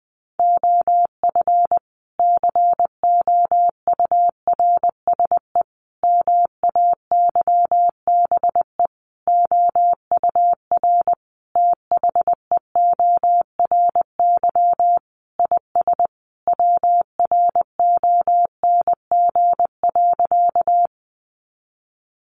Morse code, heard as a keyed tone, about 19 wpm, 700 Hz. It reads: OF COURSE MAYBE OUR THEORY IS WRONG.
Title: A Sound Of Thunder 20wpm